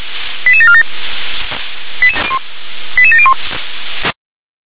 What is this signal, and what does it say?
Five-tone selective call (ZVEI), heard as quick runs of five tones, tones of about 70 ms. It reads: 88747 88701